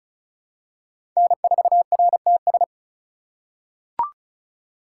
Morse code, heard as a keyed tone, about 36 wpm, 700 Hz. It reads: N4RTS
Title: USA Call Signs - 4x Character Spacing - 35wpm